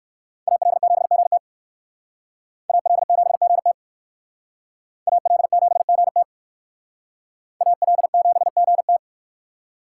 Morse code, heard as a keyed tone, about 59 wpm, 700 Hz. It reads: AL7GT AL7GT AL7GT AL7GT